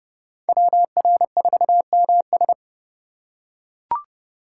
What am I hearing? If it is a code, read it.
Morse code, heard as a keyed tone, about 32 wpm, 700 Hz. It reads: WR4MS
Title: USA Call Signs - 5 Characters - 2x Character Spacing - 30wpm